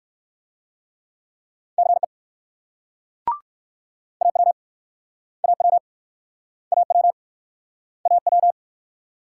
Morse code, heard as a keyed tone, about 56 wpm, 690 Hz. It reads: CE AW AW AW AW
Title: Speed-Racing Multi-letter Phonemes - 45wpm